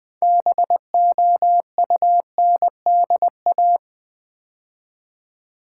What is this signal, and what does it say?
Morse code, heard as a keyed tone, about 20 wpm, 700 Hz. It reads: BOUNDA
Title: Build A Word - 2 Letters At A Time - 8-Letter Words - 20wpm